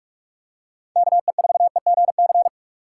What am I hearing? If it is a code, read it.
Morse code, heard as a keyed tone, about 44 wpm, 700 Hz. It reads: KE4EGC